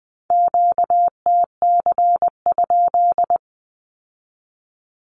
Morse code, heard as a keyed tone, about 20 wpm, 700 Hz. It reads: QTC?
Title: Q-Codes with Spoken Meaning - 20wpm